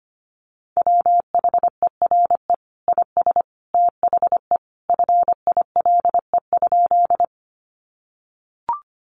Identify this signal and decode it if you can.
Morse code, heard as a keyed tone, about 25 wpm, 700 Hz. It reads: WHERE IS THE FILE?
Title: Sentences from Top 1200 Words 25wpm